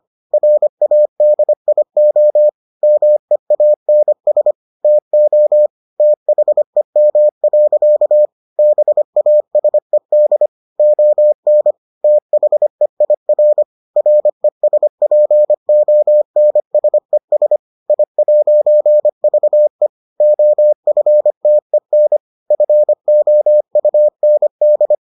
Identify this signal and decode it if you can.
Morse code, heard as a keyed tone, about 25 wpm, 600 Hz. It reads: RADIO MEANS TO THEM. BASED ON THEIR RESPONSES I'VE OFTEN FOUND